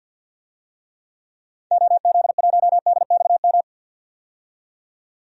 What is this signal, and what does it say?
Morse code, heard as a keyed tone, about 45 wpm, 700 Hz. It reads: OZ1DXM
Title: DXCC-100 Call Signs - 6 Characters - 50wpm